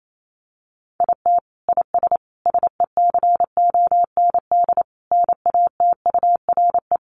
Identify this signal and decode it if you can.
Morse code, heard as a keyed tone, about 28 wpm, 700 Hz. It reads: IT IS SECOND NATURE
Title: Sentences from Top 600 Words 28wpm